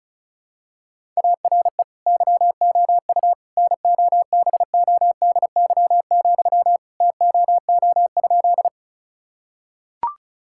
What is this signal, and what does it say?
Morse code, heard as a keyed tone, about 36 wpm, 700 Hz. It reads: ARE YOU NOBODY, TOO?
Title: Sentences from Top 1600 Words 35wpm